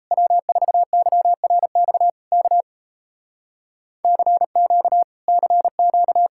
Morse code, heard as a keyed tone, about 38 wpm, 700 Hz. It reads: W4YRX K CQ CQ